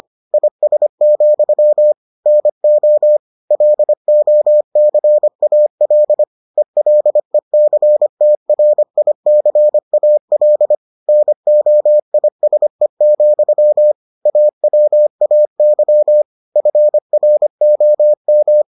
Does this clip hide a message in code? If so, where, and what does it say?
Yes: Morse code from 0.3 to 18.6 s, heard as a keyed tone, about 25 wpm, 600 Hz. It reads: IS, NO LOCAL ELECTRICAL NOISE, AWAY FROM